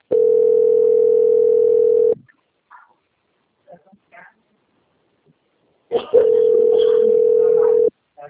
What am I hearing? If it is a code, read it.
Ringback tone (North America), 440 + 480 Hz.